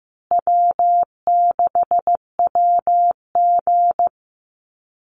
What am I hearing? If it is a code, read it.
Morse code, heard as a keyed tone, about 15 wpm, 700 Hz. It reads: W6WG
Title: Easy Call Signs Repeated 3x - 15wpm